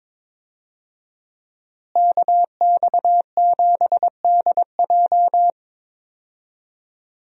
Morse code, heard as a keyed tone, about 21 wpm, 700 Hz. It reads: KX7DJ
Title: USA Call Signs - 5 Characters - Repeated 3x - 22wpm